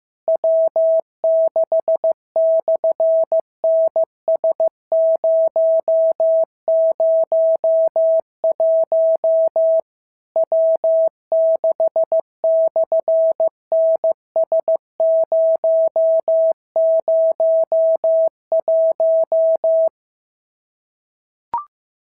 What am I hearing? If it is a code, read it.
Morse code, heard as a keyed tone, about 15 wpm, 650 Hz. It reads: W6/NS001 W6/NS001
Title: SOTA QSO Practice 15wpm